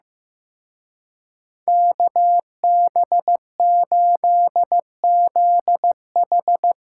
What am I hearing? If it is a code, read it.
Morse code, heard as a keyed tone, about 17 wpm, 700 Hz. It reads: KB8ZH
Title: Speed Racing - USA Call Signs 15wpm